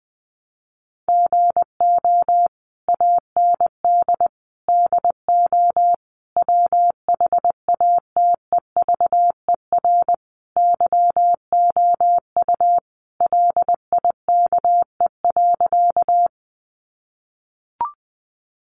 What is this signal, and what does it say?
Morse code, heard as a keyed tone, about 20 wpm, 700 Hz. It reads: GO AND DO WHATEVER YOU LIKE.
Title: Sentences from Top 1100 Words 20wpm